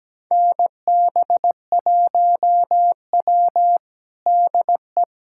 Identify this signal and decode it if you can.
Morse code, heard as a keyed tone, about 17 wpm, 700 Hz. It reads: NB1W DE
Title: USA Callsign DE Callsign - 4 Characters - 17wpm